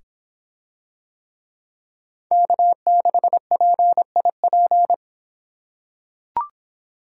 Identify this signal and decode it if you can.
Morse code, heard as a keyed tone, about 26 wpm, 700 Hz. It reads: K6PIP